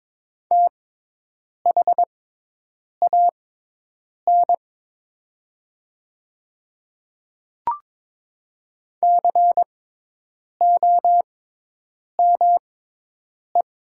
Morse code, heard as a keyed tone, about 22 wpm, spaced as for 8 wpm, 700 Hz. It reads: THAN COME
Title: 6x Character Spacing - Top 100 Words - 22wpm